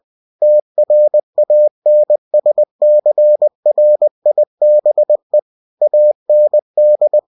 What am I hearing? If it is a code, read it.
Morse code, heard as a keyed tone, about 20 wpm, 600 Hz. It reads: TRANSCRIBE AND